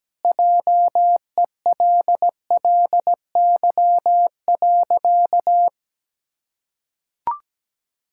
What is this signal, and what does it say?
Morse code, heard as a keyed tone, about 17 wpm, 700 Hz. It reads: JELLY.
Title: Pangram Challenge v2 - 17wpm